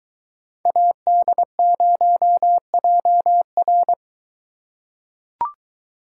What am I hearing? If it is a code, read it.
Morse code, heard as a keyed tone, about 23 wpm, 700 Hz. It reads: AD0JR